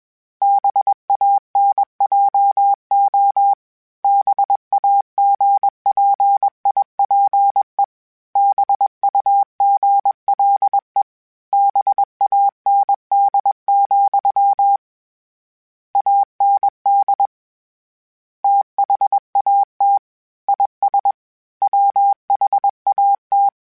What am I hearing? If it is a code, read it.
Morse code, heard as a keyed tone, about 21 wpm, 800 Hz. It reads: BANJO BAGPIPE BUGLE BAND, AND THAT IS WHAT